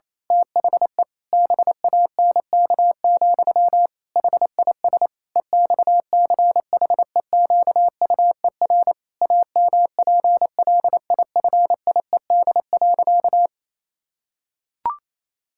Morse code, heard as a keyed tone, about 28 wpm, 700 Hz. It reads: THE BANK, HIS EXCHEQUER AMPLIFIED.
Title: Pangram Challenge v2 - 28wpm